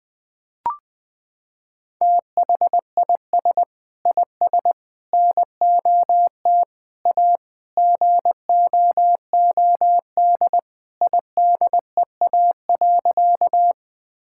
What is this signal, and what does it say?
Morse code, heard as a keyed tone, about 20 wpm, 700 Hz. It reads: THIS IS NOT A GOOD IDEA.
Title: Sentences with 4-Word Phrases - 20wpm